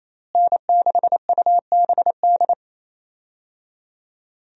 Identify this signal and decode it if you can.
Morse code, heard as a keyed tone, about 27 wpm, 700 Hz. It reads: N6UBD
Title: USA Call Signs - 5 Characters - Repeated 3x - 28wpm